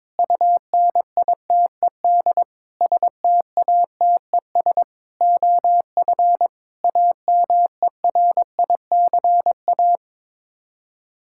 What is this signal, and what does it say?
Morse code, heard as a keyed tone, about 21 wpm, 700 Hz. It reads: UNITED STATES OF AMERICA
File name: practice / 22wpm / Sets of 8 Words - 22wpm